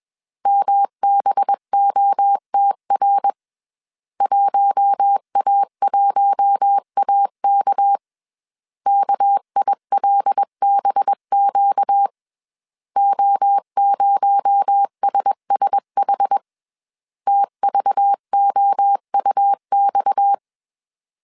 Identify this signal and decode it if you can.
Morse code, heard as a keyed tone, about 21 wpm, 800 Hz. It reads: MBOTR 1A1AK KILBQ O0SSH TVOUX
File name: CW / DET / 1 6 1500C S C 20wpm 2116